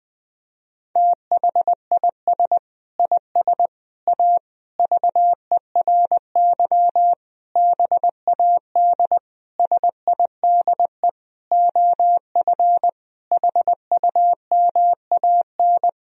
Morse code, heard as a keyed tone, about 20 wpm, 700 Hz. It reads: THIS IS A VERY BAD SIDE OF HUMAN
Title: Sentences from Top 1000 Words - Review Part 3 - 20wpm